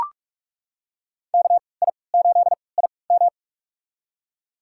Morse code, heard as a keyed tone, about 42 wpm, spaced as for 25 wpm, 700 Hz. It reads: KI8IM
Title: USA Call Signs - 3x Character Spacing - 45wpm